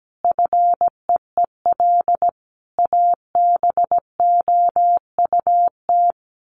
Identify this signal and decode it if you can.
Morse code, heard as a keyed tone, about 17 wpm, 700 Hz. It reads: FEEL ABOUT
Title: Sets of 5 Words 17wpm